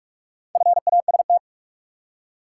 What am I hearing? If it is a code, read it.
Morse code, heard as a keyed tone, about 45 wpm, 700 Hz. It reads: FAST
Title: Four Letter Words - 45wpm